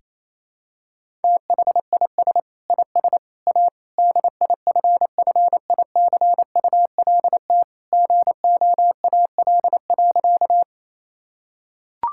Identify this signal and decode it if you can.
Morse code, heard as a keyed tone, about 28 wpm, 700 Hz. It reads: THIS IS A DIFFICULT GOAL.
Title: Sentences from Top 1000 Words - Review Part 5 - 28wpm